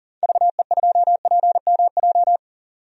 Morse code, heard as a keyed tone, about 40 wpm, 700 Hz. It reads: VE2PMJ
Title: DXCC-100 Call Signs - 6 Characters - 40wpm